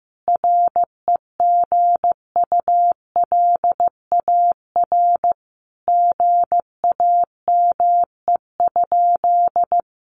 Morse code, heard as a keyed tone, about 15 wpm, 700 Hz. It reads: REGULAR GAME?
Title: Sentences from Top 900 Words 15wpm